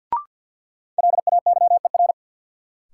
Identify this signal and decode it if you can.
Morse code, heard as a keyed tone, about 47 wpm, 700 Hz. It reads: LAYER
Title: Sentences from Top 1600 Words - New Word Drill - 50wpm